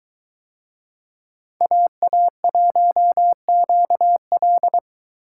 Morse code, heard as a keyed tone, about 21 wpm, 700 Hz. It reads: AA1QL